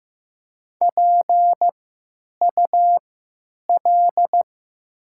Morse code, heard as a keyed tone, about 15 wpm, spaced as for 9 wpm, 700 Hz. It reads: PUL
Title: 3x Character Spacing - Top 500 Words - 15wpm